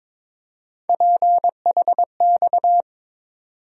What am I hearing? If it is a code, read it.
Morse code, heard as a keyed tone, about 22 wpm, 700 Hz. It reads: PHX
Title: USA Call Sign Postfixes - 3 Letters 22wpm